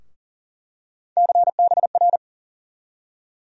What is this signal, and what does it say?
Morse code, heard as a keyed tone, about 40 wpm, 700 Hz. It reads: CBR
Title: USA Call Sign Postfixes - 3 Letters 40wpm